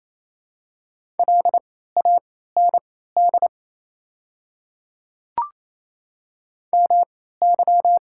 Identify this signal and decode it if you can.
Morse code, heard as a keyed tone, about 28 wpm, spaced as for 16 wpm, 700 Hz. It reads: LAND MY
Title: 3x Character Spacing - Top 300 Words - 28wpm